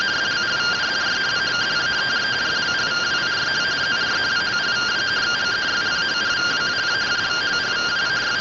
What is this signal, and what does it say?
Two-tone teleprinter signal, about 170 Hz shift.